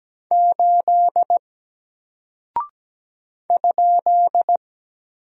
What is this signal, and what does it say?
Morse code, heard as a keyed tone, about 17 wpm, 700 Hz. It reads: 8 ?
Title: Single Character (Letter:Number:Punctuation:Prosign) 17wpm